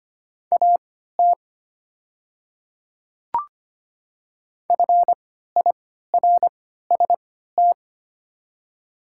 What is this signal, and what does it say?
Morse code, heard as a keyed tone, about 25 wpm, spaced as for 14 wpm, 700 Hz. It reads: AT FIRST
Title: 3x Character Spacing - Top 200 Words - 25wpm